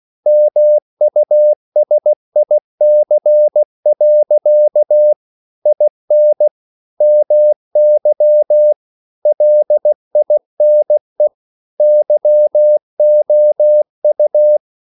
Morse code, heard as a keyed tone, about 16 wpm, 600 Hz. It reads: MUSIC. IN MY LINE YOU